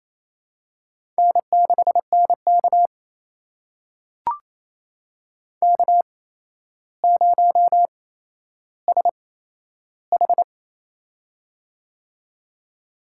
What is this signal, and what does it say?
Morse code, heard as a keyed tone, about 28 wpm, spaced as for 12 wpm, 700 Hz. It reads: N6NK K0SH
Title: USA Call Signs - 4 Characters - 8x Character Spacing - 28wpm